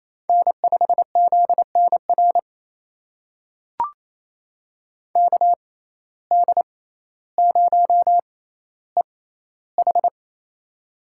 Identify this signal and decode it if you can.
Morse code, heard as a keyed tone, about 28 wpm, spaced as for 14 wpm, 700 Hz. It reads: N5ZNR KD0EH
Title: USA Call Signs - 5 Characters - 6x Character Spacing - 28wpm